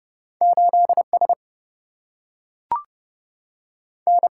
Morse code, heard as a keyed tone, about 30 wpm, 700 Hz. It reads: 8 S N
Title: Single-Letter Single-Number 30wpm